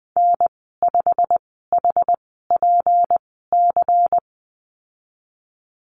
Morse code, heard as a keyed tone, about 20 wpm, spaced as for 15 wpm, 700 Hz. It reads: N5HPC
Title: 167 Lesson 27g - Number 7 - Call Signs 20wpm